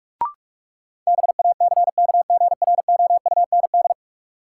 Morse code, heard as a keyed tone, about 46 wpm, 700 Hz. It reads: BACKGROUND
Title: Sentences from Top 1300 Words - New Word Drill - 45wpm